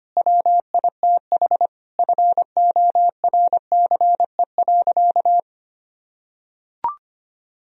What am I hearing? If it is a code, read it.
Morse code, heard as a keyed tone, about 25 wpm, 700 Hz. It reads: WITH FORCE.
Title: Sentences from Top 1000 Words - Review Part 3 - 25wpm